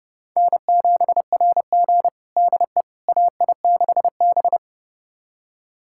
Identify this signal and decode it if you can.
Morse code, heard as a keyed tone, about 30 wpm, 700 Hz. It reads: N7RG DE AI6B
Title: USA Callsign DE Callsign - 4 Characters - 30wpm